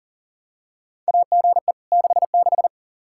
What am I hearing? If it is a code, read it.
Morse code, heard as a keyed tone, about 39 wpm, 700 Hz. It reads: AGE 66
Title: Practice QSO - Exchange 3 - More Info - 40wpm